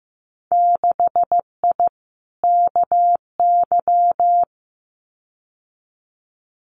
Morse code, heard as a keyed tone, about 15 wpm, 700 Hz. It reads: 6I KY
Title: Winter Field Day Exchanges 15wpm